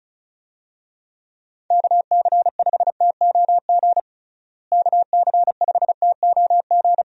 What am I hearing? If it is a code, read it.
Morse code, heard as a keyed tone, about 36 wpm, 700 Hz. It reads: KC5TOG KC5TOG
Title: USA Call Signs - 6 Characters - Repeated 2x - 35wpm